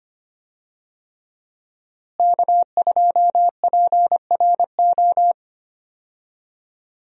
Morse code, heard as a keyed tone, about 25 wpm, 700 Hz. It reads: K2PRO
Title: USA Call Signs - 5 Characters - Repeated 3x - 25wpm